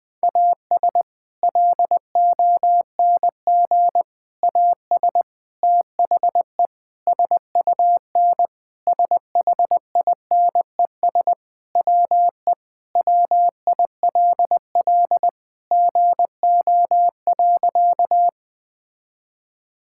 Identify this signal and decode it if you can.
Morse code, heard as a keyed tone, about 21 wpm, 700 Hz. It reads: AS LONG AS THE SUN SHINES WE WILL GO.
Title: Sentences with 4-Word Phrases - 20wpm